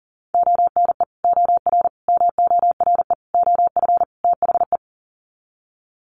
Morse code, heard as a keyed tone, about 37 wpm, 700 Hz. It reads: ONE OR MORE OF THE
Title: Sets of 5 Words Encore 40wpm